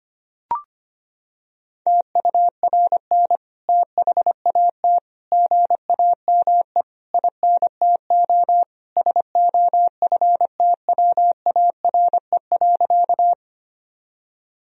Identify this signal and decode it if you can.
Morse code, heard as a keyed tone, about 25 wpm, 700 Hz. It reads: TURN THAT GAME INTO SOFTWARE.
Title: Sentences from Top 1000 Words - Review Part 3 - 25wpm